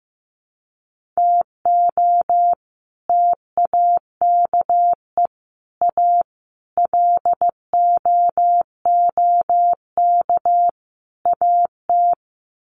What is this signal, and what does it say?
Morse code, heard as a keyed tone, about 15 wpm, 700 Hz. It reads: TO TAKE A LOOK AT
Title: Sets of 5 Words 15wpm